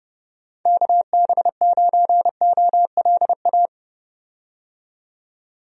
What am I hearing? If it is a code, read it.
Morse code, heard as a keyed tone, about 30 wpm, 700 Hz. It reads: KB9OLA
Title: USA Call Signs - 6 Characters - 30wpm